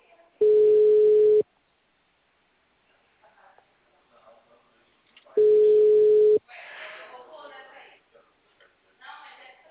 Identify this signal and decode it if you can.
Ringback tone (Europe, 425 Hz), with speech.